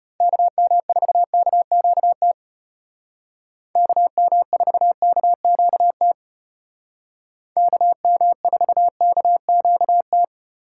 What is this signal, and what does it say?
Morse code, heard as a keyed tone, about 33 wpm, 700 Hz. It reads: KM4KQT KM4KQT KM4KQT